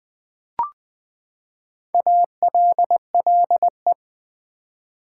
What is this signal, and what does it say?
Morse code, heard as a keyed tone, about 20 wpm, 700 Hz. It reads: ALLE
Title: Build A Word - 4 Letters At A Time - 10-Letter Words - 20wpm